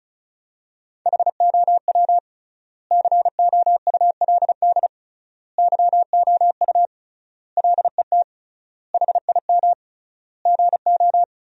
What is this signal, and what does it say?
Morse code, heard as a keyed tone, about 36 wpm, 700 Hz. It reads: HOW COULD YOU LET HIM GO
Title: 3x Word-Spacing - Sentences from Top 300 Words - 35wpm